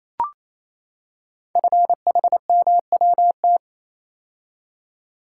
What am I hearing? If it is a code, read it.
Morse code, heard as a keyed tone, about 29 wpm, 700 Hz. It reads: FHMWT
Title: Code Groups - 5 Letters - 28wpm